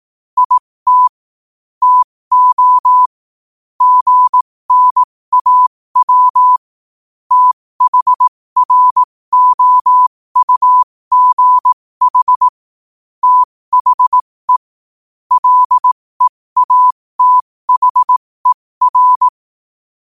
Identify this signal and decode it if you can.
Morse code, heard as a keyed tone, about 18 wpm, spaced as for 15 wpm, 1000 Hz. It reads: IT TO GNAW THROUGH THE LEATHER
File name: qotdfiles / 15wpm